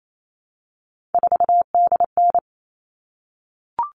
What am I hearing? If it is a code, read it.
Morse code, heard as a keyed tone, about 28 wpm, 700 Hz. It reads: 4DN
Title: All Cut Numbers - 3-Digit 28wpm